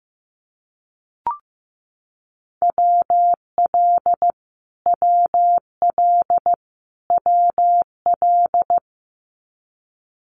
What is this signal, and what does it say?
Morse code, heard as a keyed tone, about 15 wpm, 700 Hz. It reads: WL WL WL
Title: Abbreviations Repeated 3x - 15wpm